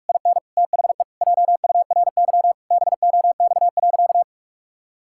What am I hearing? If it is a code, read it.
Morse code, heard as a keyed tone, about 41 wpm, 700 Hz. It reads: IN THE JURY BOX.
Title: Pangram Challenge v2 - 45wpm